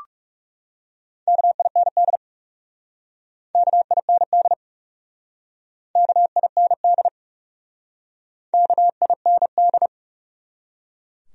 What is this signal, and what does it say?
Morse code, heard as a keyed tone, about 37 wpm, 700 Hz. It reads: KIND KIND KIND KIND
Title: Speed-Racing Top 200 Words 30wpm